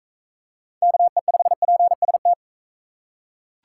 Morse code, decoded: KE5PST

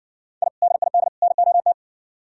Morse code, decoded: I DID NOT

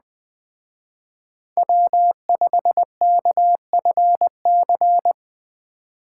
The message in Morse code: W5KFC

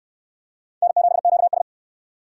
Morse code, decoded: N7QS

